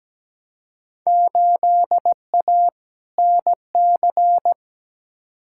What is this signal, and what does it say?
Morse code, heard as a keyed tone, about 17 wpm, 700 Hz. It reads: 8A NC
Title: Field Day Exchanges - 17wpm